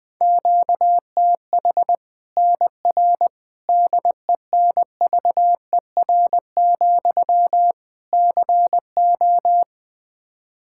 Morse code, heard as a keyed tone, about 20 wpm, 700 Hz. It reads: QTH NR DENVER, CO
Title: K1USN SST QSO Practice 20wpm